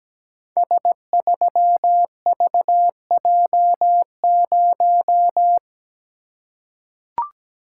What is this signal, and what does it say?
Morse code, heard as a keyed tone, about 17 wpm, 700 Hz. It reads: S3VJ0
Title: Code Groups - 5 Letter:Numbers - 17wpm